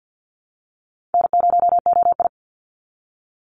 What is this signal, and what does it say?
Morse code, heard as a keyed tone, about 50 wpm, 700 Hz. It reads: N0OI
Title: NA SOTA Chaser Callsigns 50wpm